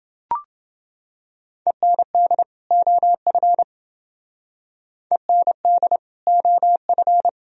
Morse code, heard as a keyed tone, about 29 wpm, 700 Hz. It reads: END OF END OF